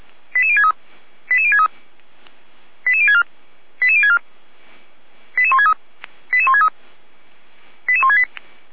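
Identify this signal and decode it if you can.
Five-tone selective call (ZVEI), heard as quick runs of five tones, tones of about 70 ms. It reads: 88063 88063 88064 88064 88163 88163 88168
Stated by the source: Five-tone ZVEI selective call received by SDR.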